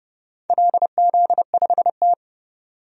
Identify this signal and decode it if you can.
Morse code, heard as a keyed tone, about 30 wpm, 700 Hz. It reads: LZ5T
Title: DXCC-100 Call Signs - 4 Characters - 30wpm